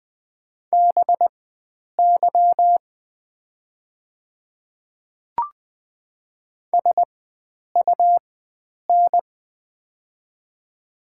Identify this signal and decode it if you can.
Morse code, heard as a keyed tone, about 20 wpm, spaced as for 9 wpm, 700 Hz. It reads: BY SUN
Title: 4x Character Spacing - Top 300 Words - 20wpm